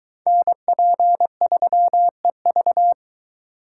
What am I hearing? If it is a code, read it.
Morse code, heard as a keyed tone, about 22 wpm, 700 Hz. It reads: NP3EV